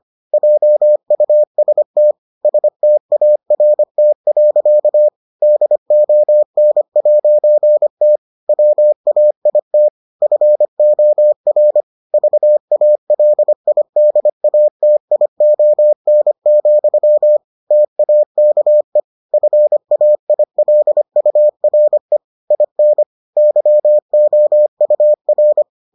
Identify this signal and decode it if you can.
Morse code, heard as a keyed tone, about 25 wpm, 600 Hz. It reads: JUST START. DON'T WAIT FOR VALIDATION, TAKE FAILURE IN YOUR